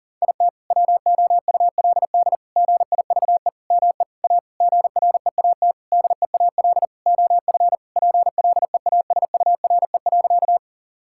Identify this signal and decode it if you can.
Morse code, heard as a keyed tone, about 40 wpm, 700 Hz. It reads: IT WOULD GIVE ME A GREAT DEAL OF PLEASURE.
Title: Sentences from Top 1600 Words 40wpm